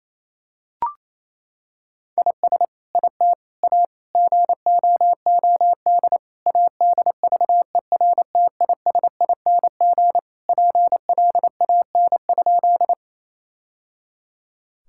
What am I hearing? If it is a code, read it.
Morse code, heard as a keyed tone, about 28 wpm, 700 Hz. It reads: IS IT A GOOD ADVERTISING PLAN?
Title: Sentences from Top 1000 Words - Review Part 6 - 28wpm